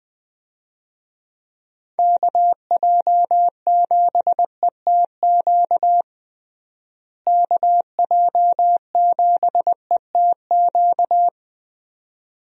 Morse code, heard as a keyed tone, about 20 wpm, 700 Hz. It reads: KJ7ETQ KJ7ETQ
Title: USA Call Signs - 6 Characters - Repeated 2x - 20wpm